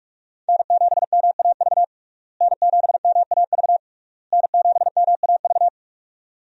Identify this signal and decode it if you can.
Morse code, heard as a keyed tone, about 42 wpm, 700 Hz. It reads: N7MAV N7MAV N7MAV
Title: USA Call Signs - 5 Characters - Repeated 3x - 45wpm